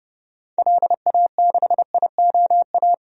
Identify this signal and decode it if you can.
Morse code, heard as a keyed tone, about 29 wpm, 690 Hz. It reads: LA6IOA
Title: DXCC-100 Call Signs - 6 Characters - 30wpm